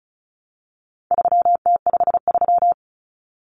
Morse code, heard as a keyed tone, about 34 wpm, 700 Hz. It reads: 3T53
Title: Common Cut Numbers - 4-Digit 35wpm